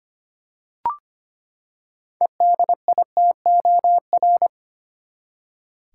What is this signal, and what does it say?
Morse code, heard as a keyed tone, about 25 wpm, 700 Hz. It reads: EDITOR
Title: Six Letter Words - 25wpm